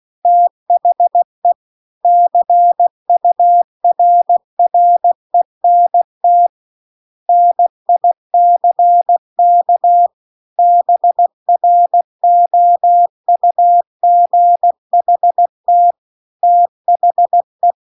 Morse code, heard as a keyed tone, about 16 wpm, 700 Hz. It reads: THE CURRENT NICK BROUGHT THE
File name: CW-The-Big-Twoheaded-River-2-16WPM